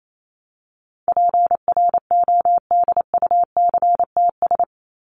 Morse code, heard as a keyed tone, about 28 wpm, 700 Hz. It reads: PRODUCTS
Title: ES:S Suffix - Top 100 Words with Base Words 28wpm